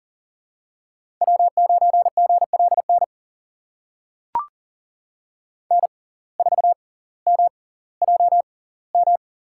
Morse code, heard as a keyed tone, about 38 wpm, spaced as for 20 wpm, 700 Hz. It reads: W9GLN N4MJM
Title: USA Call Signs - 5 Characters - 6x Character Spacing - 40wpm